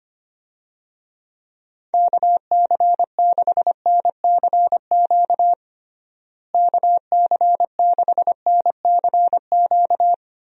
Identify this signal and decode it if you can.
Morse code, heard as a keyed tone, about 25 wpm, 700 Hz. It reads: KC6NCQ KC6NCQ
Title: USA Call Signs - 6 Characters - Repeated 2x - 25wpm